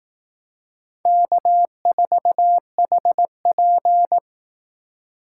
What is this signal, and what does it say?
Morse code, heard as a keyed tone, about 18 wpm, 700 Hz. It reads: K4HP
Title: Speed Racing - USA 1x2 Call Signs 15wpm